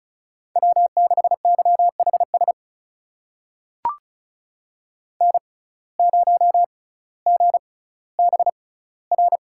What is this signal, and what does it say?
Morse code, heard as a keyed tone, about 36 wpm, spaced as for 18 wpm, 700 Hz. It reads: W6YHS N0GBR
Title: USA Call Signs - 5 Characters - 6x Character Spacing - 35wpm